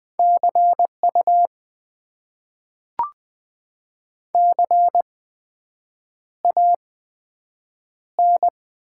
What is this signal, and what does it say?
Morse code, heard as a keyed tone, about 20 wpm, spaced as for 7 wpm, 700 Hz. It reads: CU CAN
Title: Top 90 QSO Elements - 8x Character Spacing - 20wpm